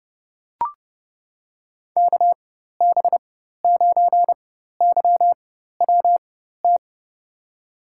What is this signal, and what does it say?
Morse code, heard as a keyed tone, about 30 wpm, spaced as for 14 wpm, 700 Hz. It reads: KB9YWT
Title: USA Call Signs - 4x Character Spacing - 30wpm